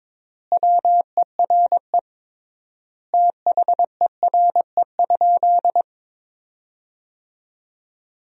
Morse code, heard as a keyed tone, about 22 wpm, 700 Hz. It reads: WERE THERE?
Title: 3x Word-Spacing - Sentences from Top 500 Words - 22wpm